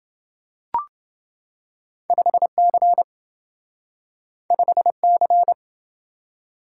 Morse code, heard as a keyed tone, about 27 wpm, 700 Hz. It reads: 5C 5C